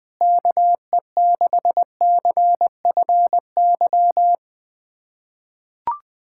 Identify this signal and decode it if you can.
Morse code, heard as a keyed tone, about 21 wpm, 700 Hz. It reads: KE6CFY
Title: USA Callsign DE Callsign 20wpm